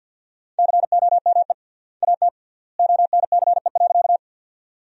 Morse code, heard as a keyed tone, about 50 wpm, 700 Hz. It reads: COME AT ONCE.
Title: 3x Word-Spacing - Sentences from Top 300 Words - 50wpm